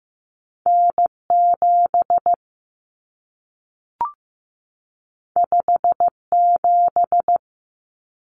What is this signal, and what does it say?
Morse code, heard as a keyed tone, about 15 wpm, 700 Hz. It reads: N7 57
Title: Common Cut Numbers - 2-Digit 15wpm